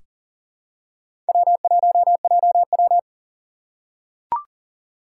Morse code, heard as a keyed tone, about 40 wpm, 700 Hz. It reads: W1JW
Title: USA Call Signs - 4 Characters - 2x Character Spacing - 40wpm